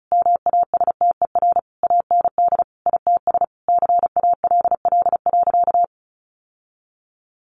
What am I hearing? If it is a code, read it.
Morse code, heard as a keyed tone, about 34 wpm, 700 Hz. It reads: MASTER AND ITS CALL.
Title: Sentences from Top 1300 Words 35wpm